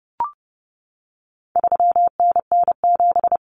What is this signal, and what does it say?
Morse code, heard as a keyed tone, about 30 wpm, 700 Hz. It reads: 3NN7
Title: Common Cut Numbers - 4-Digit 30wpm